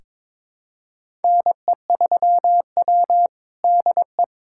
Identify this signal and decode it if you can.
Morse code, heard as a keyed tone, about 22 wpm, 700 Hz. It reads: NE3W DE
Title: USA Callsign DE Callsign - 4 Characters - 22wpm